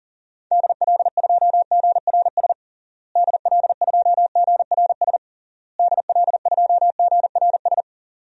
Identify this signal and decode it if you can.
Morse code, heard as a keyed tone, about 40 wpm, 700 Hz. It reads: DL2GRS DL2GRS DL2GRS
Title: DXCC-100 Call Signs - 6 Characters - Repeated 3x - 40wpm